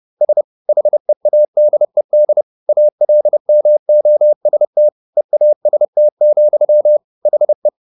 Morse code, heard as a keyed tone, about 30 wpm, 600 Hz. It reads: S HEADED ALMOST EAST, HE